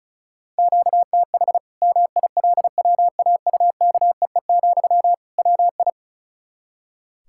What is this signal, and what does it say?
Morse code, heard as a keyed tone, about 36 wpm, 700 Hz. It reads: QTH MILWAUKEE, WI
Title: Practice QSO - Exchange 1 - Beginning - 35wpm